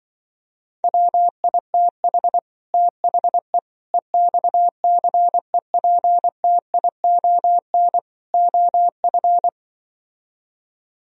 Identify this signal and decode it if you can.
Morse code, heard as a keyed tone, about 25 wpm, 700 Hz. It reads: WITH THE EXCEPTION OF
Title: Speed-Racing - Sets of 4 Words 20wpm